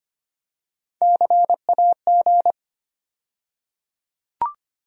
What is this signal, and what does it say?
Morse code, heard as a keyed tone, about 25 wpm, 700 Hz. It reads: CAG
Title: USA Call Sign Postfixes - 3 Letters 25wpm